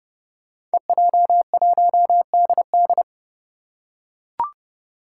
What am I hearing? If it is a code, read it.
Morse code, heard as a keyed tone, about 31 wpm, 700 Hz. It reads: EJ1DD
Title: Five Challenging Letter-Number Combinations 30wpm